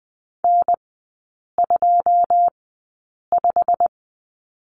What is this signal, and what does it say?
Morse code, heard as a keyed tone, about 20 wpm, spaced as for 8 wpm, 700 Hz. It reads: N25
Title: 087 Lesson 15e - Number 5 - Three Characters 20wpm